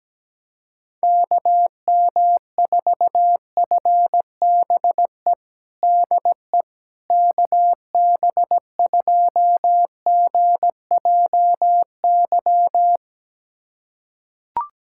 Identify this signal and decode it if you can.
Morse code, heard as a keyed tone, about 17 wpm, 700 Hz. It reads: KM4FBE DE KB2GJY
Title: USA Callsign DE Callsign - 6 Characters - 17wpm